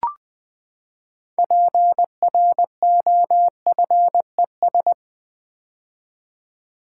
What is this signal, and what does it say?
Morse code, heard as a keyed tone, about 20 wpm, 700 Hz. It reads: PROFES
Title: Build A Word - 6 Letters At A Time - 12-Letter Words - 20wpm